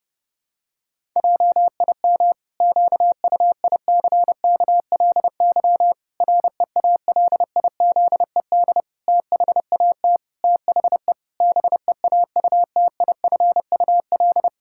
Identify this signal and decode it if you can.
Morse code, heard as a keyed tone, about 30 wpm, 700 Hz. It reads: JIM QUICKLY REALIZED THAT THE BEAUTIFUL
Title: Pangram Challenge v2 - 30wpm